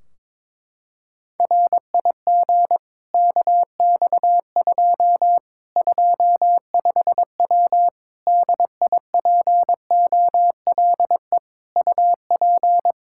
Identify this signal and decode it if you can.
Morse code, heard as a keyed tone, about 22 wpm, 700 Hz. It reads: RIG KX2 25W DIPOLE UP
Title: Practice QSO - Exchange 2 - Info - 22wpm